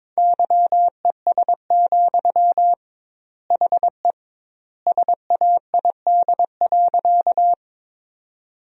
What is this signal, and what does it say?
Morse code, heard as a keyed tone, about 22 wpm, 700 Hz. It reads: YES, HE SAID.
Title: 2x Word-Spacing - Sentences from Top 500 Words - 22wpm